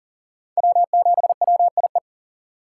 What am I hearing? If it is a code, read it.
Morse code, heard as a keyed tone, about 40 wpm, 700 Hz. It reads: W7WIE
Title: USA Call Signs - 2x Character Spacing - 40wpm